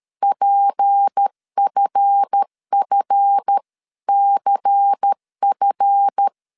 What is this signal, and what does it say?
Morse code, heard as a keyed tone, about 13 wpm, 800 Hz. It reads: PFFCF
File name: LEZIONI / 7 GF9Z / 7 GF 12wpm 2122